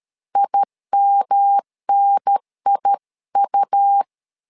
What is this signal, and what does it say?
Morse code, heard as a keyed tone, about 13 wpm, 800 Hz. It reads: IMNIU
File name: LEZIONI / 3 MWRO / 3 M 12wpm 1541